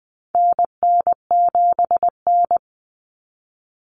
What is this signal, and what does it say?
Morse code, heard as a keyed tone, about 20 wpm, 700 Hz. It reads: NN7N
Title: Common Cut Numbers - 4-Digit 20wpm